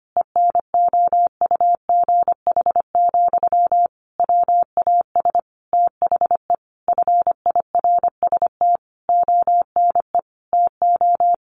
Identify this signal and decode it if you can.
Morse code, heard as a keyed tone, about 25 wpm, 700 Hz. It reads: ENOUGH, WAS THE FIRST ONE TO
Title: Sentences from Top 1300 Words 25wpm